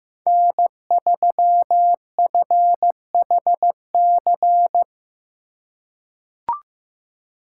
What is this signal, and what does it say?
Morse code, heard as a keyed tone, about 15 wpm, 700 Hz. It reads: N3FHC
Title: USA Call Signs - 5 Characters - 2x Character Spacing - 15wpm